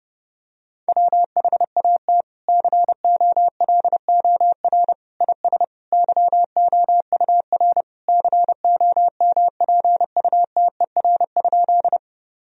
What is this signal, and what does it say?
Morse code, heard as a keyed tone, about 30 wpm, 700 Hz. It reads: WHAT COLOR IS YOUR COMPUTER?
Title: Sentences from Top 1000 Words - Review Part 3 - 30wpm